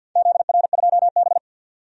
Morse code, heard as a keyed tone, about 49 wpm, 700 Hz. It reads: ZR2B
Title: DXCC-100 Call Signs - 4 Characters - 50wpm